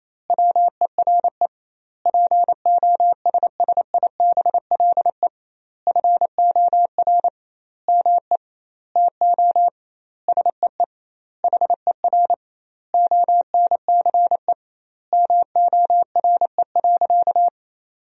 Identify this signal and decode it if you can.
Morse code, heard as a keyed tone, about 28 wpm, 700 Hz. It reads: WERE POSSIBLE FOR ME TO SEE HER ONCE MORE.
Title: 2x Word-Spacing - Sentences from Top 500 Words - 28wpm